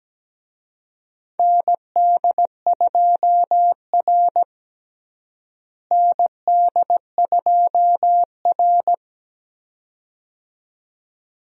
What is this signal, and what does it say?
Morse code, heard as a keyed tone, about 17 wpm, 700 Hz. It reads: ND2R ND2R
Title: USA Call Signs - 4 Characters - Repeated 2x - 17wpm